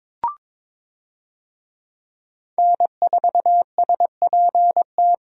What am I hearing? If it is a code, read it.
Morse code, heard as a keyed tone, about 23 wpm, 700 Hz. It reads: N4SPT
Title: USA Call Signs - 5 Characters - Repeated 2x - 22wpm